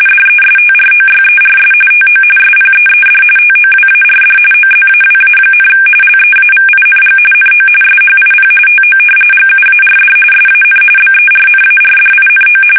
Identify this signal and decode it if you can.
RTTY, about 45.45 baud. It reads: DESTROYERS TO SPEED TOWARD OZAWA, LEAVING NO SHIP TO GUARD SAN BERNARDINO
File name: fsk850 / battleshipnj